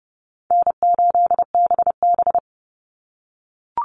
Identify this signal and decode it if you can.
Morse code, heard as a keyed tone, about 30 wpm, 700 Hz. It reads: N8BB
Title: NA SOTA Chaser Callsigns 30wpm